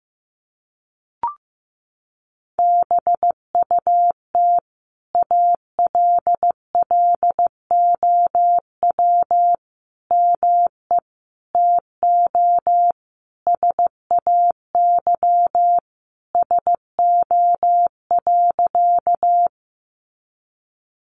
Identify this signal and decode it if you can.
Morse code, heard as a keyed tone, about 14 wpm, 700 Hz. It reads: BUT ALLOW ME TO SAY SO.
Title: Sentences from Top 700 Words 15wpm